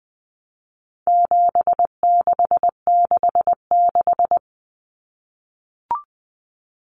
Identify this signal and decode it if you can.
Morse code, heard as a keyed tone, about 20 wpm, 700 Hz. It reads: 7666
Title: Numbers - 4-Digit 20wpm